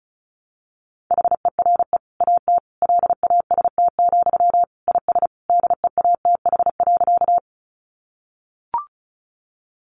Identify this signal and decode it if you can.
Morse code, heard as a keyed tone, about 34 wpm, 700 Hz. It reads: HERE AT LAST, IS DEATH.
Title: Sentences from Top 1100 Words 35wpm